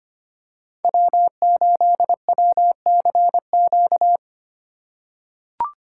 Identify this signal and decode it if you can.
Morse code, heard as a keyed tone, about 25 wpm, 700 Hz. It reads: W8WCQ
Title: USA Call Signs - 5 Characters - 2x Character Spacing - 25wpm